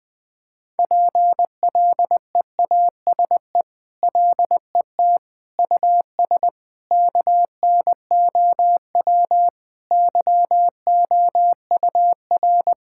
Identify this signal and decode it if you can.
Morse code, heard as a keyed tone, about 19 wpm, 700 Hz. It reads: PLEASE LET US KNOW YOUR
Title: Sentences with 8-Word Phrases - 20wpm